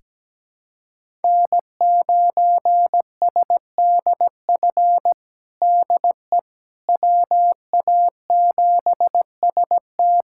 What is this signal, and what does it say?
Morse code, heard as a keyed tone, about 17 wpm, 700 Hz. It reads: N9SDF DE WA7ST
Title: USA Callsign DE Callsign - 5 Characters - 17wpm